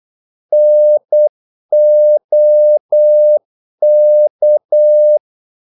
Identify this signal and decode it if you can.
Morse code, heard as a keyed tone, about 8 wpm, 600 Hz. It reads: NOK
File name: Korte ord DK 8wpm